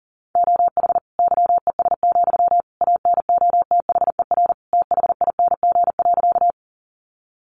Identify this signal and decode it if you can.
Morse code, heard as a keyed tone, about 40 wpm, 700 Hz. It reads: OH YES, ANOTHER THING.
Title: Sentences from Top 600 Words 40wpm